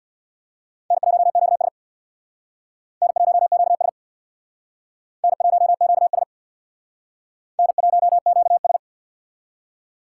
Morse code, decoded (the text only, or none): N1QS N1QS N1QS N1QS